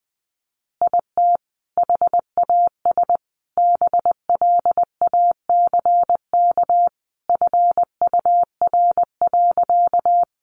Morse code, heard as a keyed tone, about 20 wpm, 700 Hz. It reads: IT HAS BLACK FUR.
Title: Easy Phrases 20wpm